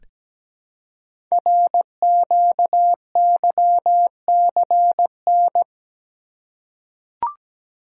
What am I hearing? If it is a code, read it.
Morse code, heard as a keyed tone, about 17 wpm, 700 Hz. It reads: RQYCN
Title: Code Groups - 5 Letters - 17wpm